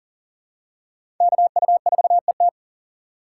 Morse code, heard as a keyed tone, about 39 wpm, 700 Hz. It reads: KU4ET
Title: USA Call Signs - 5 Characters - 4x Character Spacing - 40wpm